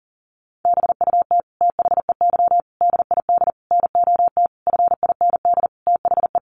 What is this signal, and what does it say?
Morse code, heard as a keyed tone, about 40 wpm, 700 Hz. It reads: BUT THEY DID NOT FIND THE
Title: Sentences from Top 1400 Words 40wpm